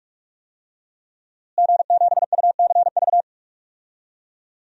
Morse code, heard as a keyed tone, about 42 wpm, 700 Hz. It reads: G7UCV
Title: DXCC-100 Call Signs - 5 Characters - Repeated 2x - 45wpm